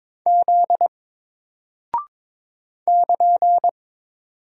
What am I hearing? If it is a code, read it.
Morse code, heard as a keyed tone, about 22 wpm, 700 Hz. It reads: Z <KN>
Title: Single Character (Letter:Number:Punctuation:Prosign) 22wpm